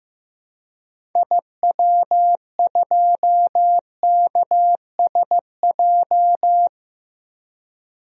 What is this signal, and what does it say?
Morse code, heard as a keyed tone, about 15 wpm, 700 Hz. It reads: IW2KSJ
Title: DXCC-100 Call Signs - 6 Characters - Repeated 2x - 15wpm